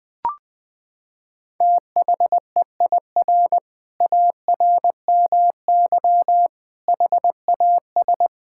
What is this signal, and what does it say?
Morse code, heard as a keyed tone, about 20 wpm, 700 Hz. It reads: THEIR ARMY HAS
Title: Sentences from Top 1000 Words - Review Part 5 - 20wpm